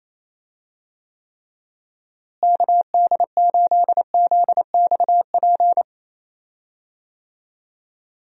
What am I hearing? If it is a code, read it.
Morse code, heard as a keyed tone, about 28 wpm, 700 Hz. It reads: KD8ZXP
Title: USA Call Signs - 6 Characters - Repeated 3x - 28wpm